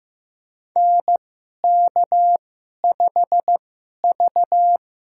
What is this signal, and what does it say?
Morse code, heard as a keyed tone, about 15 wpm, spaced as for 11 wpm, 700 Hz. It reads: NK5V
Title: USA Call Signs - 4 Characters - 2x Character Spacing - 15wpm